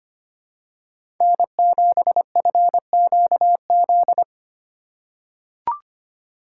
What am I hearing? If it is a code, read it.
Morse code, heard as a keyed tone, about 25 wpm, 700 Hz. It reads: N7FQZ
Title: USA Call Signs - 5 Characters - 3x Character Spacing - 25wpm